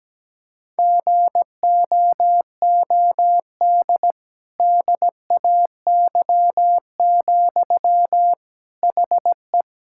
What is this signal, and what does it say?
Morse code, heard as a keyed tone, about 17 wpm, 700 Hz. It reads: GOOD DAY, HE
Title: Sentences from Top 1600 Words 17wpm